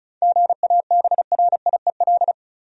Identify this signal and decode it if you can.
Morse code, heard as a keyed tone, about 34 wpm, 700 Hz. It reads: GABRIEL
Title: USA Common Names - 35wpm